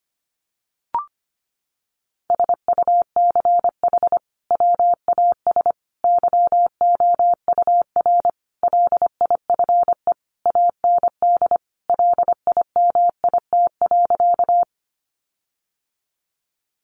Morse code, heard as a keyed tone, about 25 wpm, 700 Hz. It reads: SUCH WAS YOUR LIFE AND LIMIT.
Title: Sentences from Top 1500 Words 25wpm